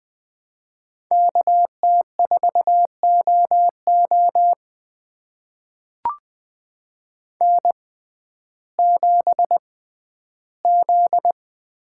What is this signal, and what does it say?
Morse code, heard as a keyed tone, about 20 wpm, spaced as for 12 wpm, 700 Hz. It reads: KT4OO N7Z
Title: USA Call Signs - 5 Characters - 6x Character Spacing - 20wpm